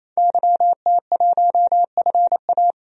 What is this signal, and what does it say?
Morse code, heard as a keyed tone, about 28 wpm, 700 Hz. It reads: YT1FA
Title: DXCC-100 Call Signs - 5 Characters - 28wpm